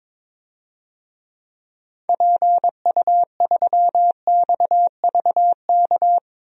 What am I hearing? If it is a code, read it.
Morse code, heard as a keyed tone, about 22 wpm, 700 Hz. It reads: PU3XVK
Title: DXCC-100 Call Signs - 6 Characters - 22wpm